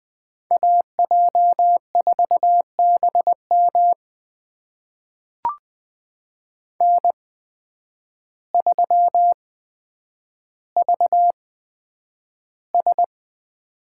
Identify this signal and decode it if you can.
Morse code, heard as a keyed tone, about 20 wpm, spaced as for 9 wpm, 700 Hz. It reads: AJ4BM N3VS
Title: USA Call Signs - 8x Character Spacing - 20wpm